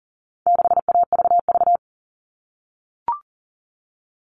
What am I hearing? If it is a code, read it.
Morse code, heard as a keyed tone, about 40 wpm, 700 Hz. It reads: 6AVV
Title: All Cut Numbers - 4-Digit 40wpm